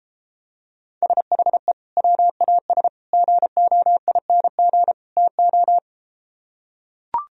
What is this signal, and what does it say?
Morse code, heard as a keyed tone, about 33 wpm, 700 Hz. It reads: SHE WAS GOING TO